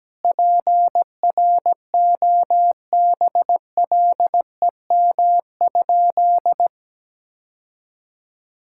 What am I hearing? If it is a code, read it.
Morse code, heard as a keyed tone, about 17 wpm, 700 Hz. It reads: PROBLEM?
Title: Sentences from Top 1000 Words - Review Part 6 - 17wpm